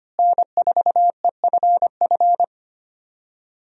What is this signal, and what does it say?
Morse code, heard as a keyed tone, about 25 wpm, 700 Hz. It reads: N4EFF
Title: USA Call Signs - 5 Characters - Repeated 3x - 25wpm